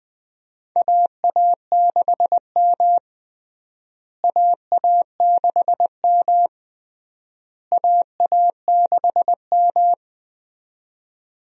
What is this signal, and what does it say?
Morse code, heard as a keyed tone, about 21 wpm, 700 Hz. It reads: AA6M AA6M AA6M
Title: USA Call Signs - 4 Characters - Repeated 3x - 20wpm